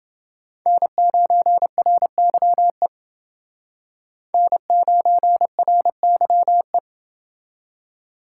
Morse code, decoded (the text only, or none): N9RYE N9RYE